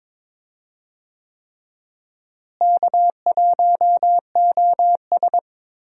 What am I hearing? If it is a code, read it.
Morse code, heard as a keyed tone, about 22 wpm, 700 Hz. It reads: K1OS
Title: USA Call Signs - 4 Characters - Repeated 3x - 22wpm